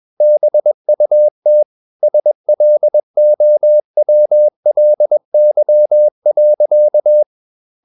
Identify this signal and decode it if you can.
Morse code, heard as a keyed tone, about 21 wpm, 600 Hz. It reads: BUT SLOWLY.
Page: Romanen Never af Ken Follet på engelsk.